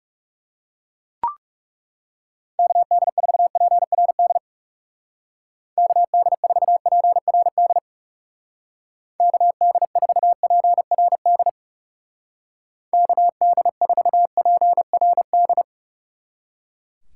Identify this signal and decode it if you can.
Morse code, heard as a keyed tone, about 39 wpm, 700 Hz. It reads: KD4PRD KD4PRD KD4PRD KD4PRD